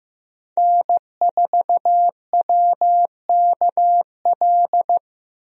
Morse code, heard as a keyed tone, about 15 wpm, 700 Hz. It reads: N4WKL
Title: USA Call Signs - 5 Characters - Repeated 3x - 15wpm